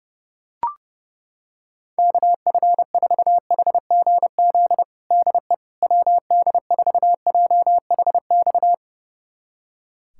Morse code, heard as a keyed tone, about 32 wpm, 700 Hz. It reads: KF4HGZ DE WD4JHX
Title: USA Callsign DE Callsign - 6 Characters - 30wpm